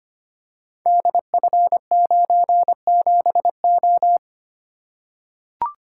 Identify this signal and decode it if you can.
Morse code, heard as a keyed tone, about 25 wpm, 700 Hz. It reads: DF97O
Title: Code Groups - 5 Letter:Numbers - 25wpm